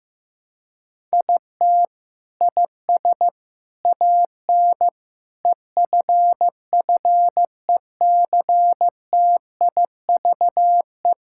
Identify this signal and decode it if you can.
Morse code, heard as a keyed tone, about 15 wpm, 700 Hz. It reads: IT IS AN EFFECTIVE
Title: Sentences from Top 1000 Words - Review Part 6 - 15wpm